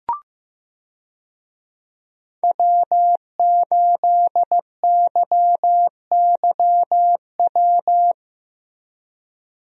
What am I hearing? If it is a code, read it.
Morse code, heard as a keyed tone, about 15 wpm, 700 Hz. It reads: W8YYW
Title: USA Call Signs - 5 Characters - Repeated 3x - 15wpm